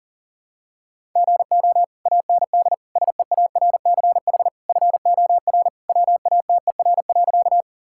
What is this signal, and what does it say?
Morse code, heard as a keyed tone, about 40 wpm, 700 Hz. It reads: GO AND SEARCH FOR WATER.
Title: Sentences from Top 1000 Words - Review Part 5 - 40wpm